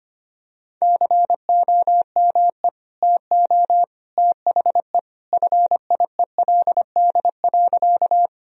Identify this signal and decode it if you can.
Morse code, heard as a keyed tone, about 25 wpm, 700 Hz. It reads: COME TO THE FIELD.
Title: Sentences from Top 1000 Words - Review Part 2 - 25wpm